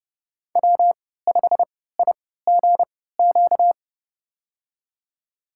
Morse code, heard as a keyed tone, about 30 wpm, spaced as for 17 wpm, 700 Hz. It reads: W5IGQ
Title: USA Call Signs - 5 Characters - 3x Character Spacing - 30wpm